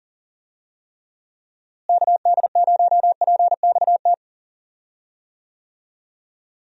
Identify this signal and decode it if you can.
Morse code, heard as a keyed tone, about 40 wpm, 700 Hz. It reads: KD0PXT
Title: USA Call Signs - 6 Characters - Repeated 2x - 40wpm